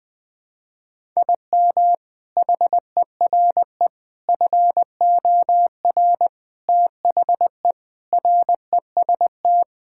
Morse code, heard as a keyed tone, about 21 wpm, 700 Hz. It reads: IM HERE FOR THE REST
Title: Sentences with 4-Word Phrases - 20wpm